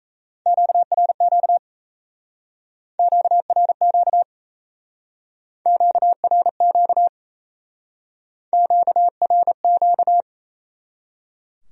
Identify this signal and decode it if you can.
Morse code, heard as a keyed tone, about 36 wpm, 700 Hz. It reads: QRQ QRQ QRQ QRQ